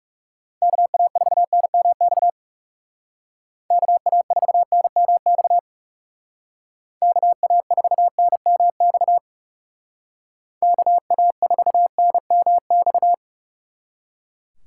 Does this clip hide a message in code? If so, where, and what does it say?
Yes: Morse code from 0.6 to 13.1 s, heard as a keyed tone, about 39 wpm, 700 Hz. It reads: KA4NMX KA4NMX KA4NMX KA4NMX